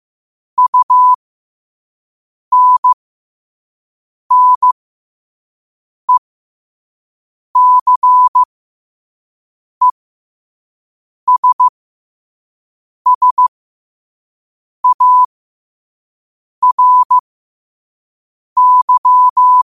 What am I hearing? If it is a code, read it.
Morse code, heard as a keyed tone, about 15 wpm, spaced as for 5 wpm, 1000 Hz. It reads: UNNECESSARY
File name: qotdfiles / 5wpm